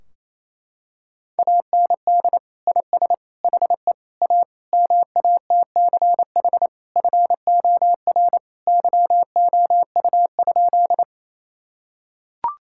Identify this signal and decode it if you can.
Morse code, heard as a keyed tone, about 28 wpm, 700 Hz. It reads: AND IS HE A MATCH FOR YOU?
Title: Sentences from Top 1600 Words 28wpm